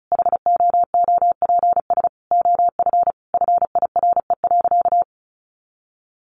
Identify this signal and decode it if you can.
Morse code, heard as a keyed tone, about 34 wpm, 700 Hz. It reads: HOOPS OF FIRE.
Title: Easy Phrases 35wpm